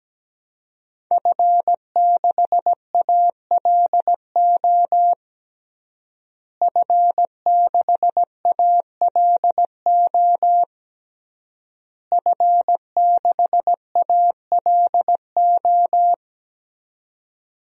Morse code, heard as a keyed tone, about 17 wpm, 700 Hz. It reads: F6ALO F6ALO F6ALO
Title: DXCC-100 Call Signs - 5 Characters - Repeated 3x - 17wpm